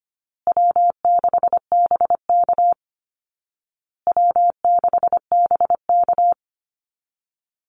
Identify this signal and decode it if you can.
Morse code, heard as a keyed tone, about 25 wpm, 700 Hz. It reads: W6BK W6BK
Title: Easy Call Signs Repeated 3x - 25wpm